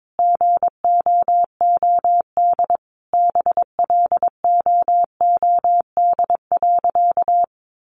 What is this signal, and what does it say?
Morse code, heard as a keyed tone, about 22 wpm, 700 Hz. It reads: GOOD BLOOD.
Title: Sentences from Top 1100 Words 22wpm